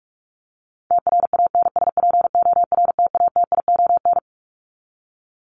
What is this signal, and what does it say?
Morse code, heard as a keyed tone, about 44 wpm, 700 Hz. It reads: TRANSPORTATION
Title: Top 1501-1750 Words 45wpm